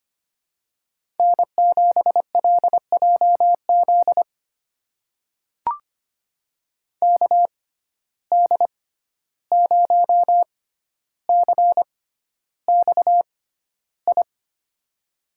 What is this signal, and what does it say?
Morse code, heard as a keyed tone, about 25 wpm, spaced as for 12 wpm, 700 Hz. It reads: N7LJZ KD0CXI
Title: USA Call Signs - 6x Character Spacing - 25wpm